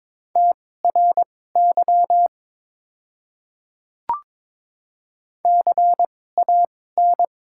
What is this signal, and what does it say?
Morse code, heard as a keyed tone, about 22 wpm, spaced as for 16 wpm, 700 Hz. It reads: TRY CAN
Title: 2x Character Spacing - Top 300 Words - 22wpm